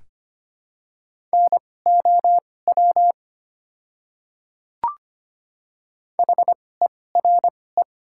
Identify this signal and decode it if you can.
Morse code, heard as a keyed tone, about 25 wpm, spaced as for 18 wpm, 700 Hz. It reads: NOW HERE
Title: 2x Character Spacing - Top 200 Words - 25wpm